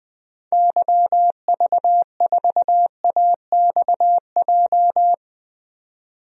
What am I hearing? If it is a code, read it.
Morse code, heard as a keyed tone, about 20 wpm, 700 Hz. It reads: YV4AXJ
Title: DXCC-100 Call Signs - 6 Characters - 20wpm